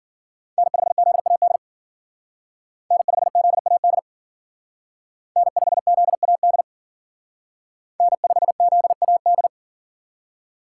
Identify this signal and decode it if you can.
Morse code, heard as a keyed tone, about 49 wpm, 700 Hz. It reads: N5ZAD N5ZAD N5ZAD N5ZAD
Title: Speed Racing - USA 1x3 Call Signs 40wpm